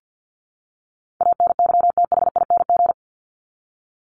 Morse code, heard as a keyed tone, about 48 wpm, 700 Hz. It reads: ANYTHING
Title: Top 751-1000 Words 50wpm